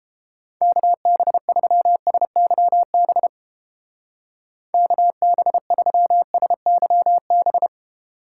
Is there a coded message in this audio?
Yes: Morse code from 0.6 to 7.7 s, heard as a keyed tone, about 31 wpm, 700 Hz. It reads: KB3SYB KB3SYB